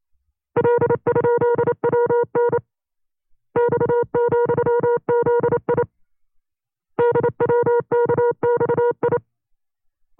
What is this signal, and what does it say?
Morse code, heard as a keyed tone, about 28 wpm, 480 Hz. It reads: L?WN X,ZI DWKXI
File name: FreeFiles / CW-practice / 25WPM-side2